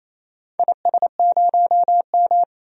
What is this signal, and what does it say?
Morse code, heard as a keyed tone, about 28 wpm, 700 Hz. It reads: IS0M
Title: DXCC-100 Call Signs - 4 Characters - 28wpm